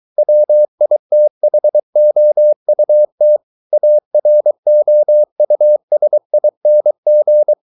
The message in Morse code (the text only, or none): WITHOUT AROUSING